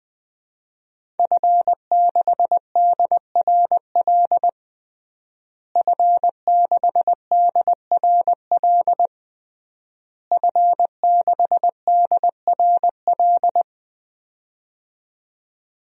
Morse code, heard as a keyed tone, about 20 wpm, 700 Hz. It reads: F6DRL F6DRL F6DRL
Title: DXCC-100 Call Signs - 5 Characters - Repeated 3x - 20wpm